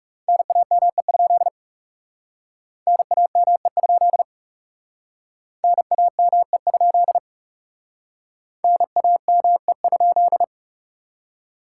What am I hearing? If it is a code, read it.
Morse code, heard as a keyed tone, about 39 wpm, 700 Hz. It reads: NAME? NAME? NAME? NAME?